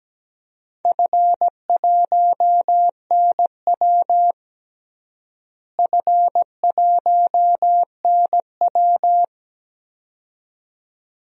Morse code, heard as a keyed tone, about 17 wpm, 700 Hz. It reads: F1NW F1NW
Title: DXCC-100 Call Signs - 4 Characters - Repeated 2x - 17wpm